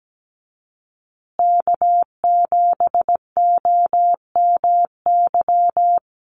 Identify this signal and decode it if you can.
Morse code, heard as a keyed tone, about 17 wpm, 700 Hz. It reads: K7OMY
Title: USA Call Signs Repeated 3x 17wpm